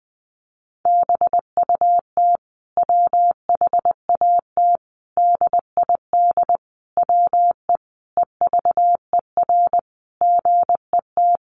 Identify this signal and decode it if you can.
Morse code, heard as a keyed tone, about 19 wpm, 700 Hz. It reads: BUT WHAT DID WE EVER GET
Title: Sentences from Top 400 Words 20wpm